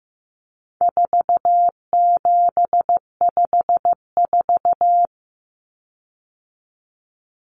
Morse code, heard as a keyed tone, about 15 wpm, 700 Hz. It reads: 4754
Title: Common Cut Numbers - 4-Digit 15wpm